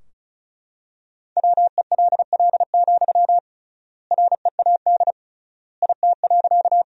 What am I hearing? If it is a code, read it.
Morse code, heard as a keyed tone, about 36 wpm, 700 Hz. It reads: WELL, READ IT.
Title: 3x Word-Spacing - Sentences from Top 200 Words - 35wpm